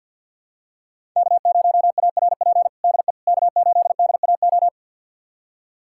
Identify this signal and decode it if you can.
Morse code, heard as a keyed tone, about 53 wpm, 700 Hz. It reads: K0ARP DE K8DAO
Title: USA Callsign DE Callsign - 5 Characters - 50wpm